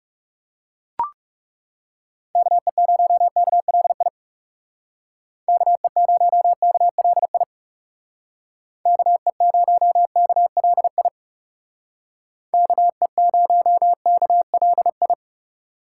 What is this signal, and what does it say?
Morse code, heard as a keyed tone, about 39 wpm, 700 Hz. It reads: KE0KLI KE0KLI KE0KLI KE0KLI